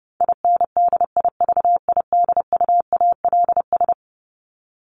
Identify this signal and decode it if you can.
Morse code, heard as a keyed tone, about 30 wpm, 700 Hz. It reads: INDIVIDUALS
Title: ES:S Suffix - Top 100 Words with Base Words 30wpm